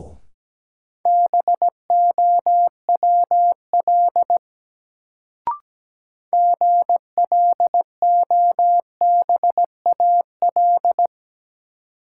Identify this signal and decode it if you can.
Morse code, heard as a keyed tone, about 17 wpm, 700 Hz. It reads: BOWL GLOBAL
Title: Sentences from Top 1200 Words - New Word Drill - 17wpm